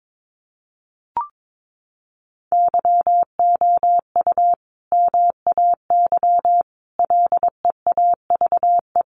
Morse code, heard as a keyed tone, about 22 wpm, 700 Hz. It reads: YOU MAY LEAVE
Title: Sentences from Top 400 Words 22wpm